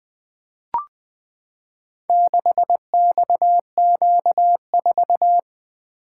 Morse code, heard as a keyed tone, about 20 wpm, 700 Hz. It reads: 6XQ4
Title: Four Letter-Number Combinations - 20wpm